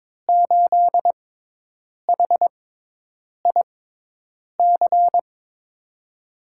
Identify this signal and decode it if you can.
Morse code, heard as a keyed tone, about 22 wpm, spaced as for 8 wpm, 700 Hz. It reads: 8HIC
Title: USA Call Signs - 6x Character Spacing - 22wpm